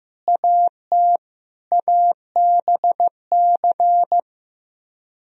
Morse code, heard as a keyed tone, about 15 wpm, 700 Hz. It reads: AT ABC
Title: Speed-Racing - Sets of 5 Words 15wpm